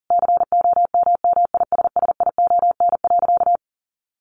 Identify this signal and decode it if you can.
Morse code, heard as a keyed tone, about 40 wpm, 700 Hz. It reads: COMMISSION.
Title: Sentences from Top 1400 Words 40wpm